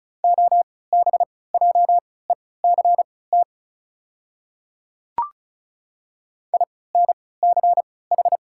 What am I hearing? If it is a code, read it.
Morse code, heard as a keyed tone, about 36 wpm, spaced as for 20 wpm, 700 Hz. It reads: OBJECT INCH
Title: 3x Character Spacing - Top 500 Words - 35wpm